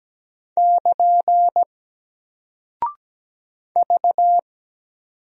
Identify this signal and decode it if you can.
Morse code, heard as a keyed tone, about 17 wpm, 700 Hz. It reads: <KN> V
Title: Single Character (Letter:Number:Punctuation:Prosign) 17wpm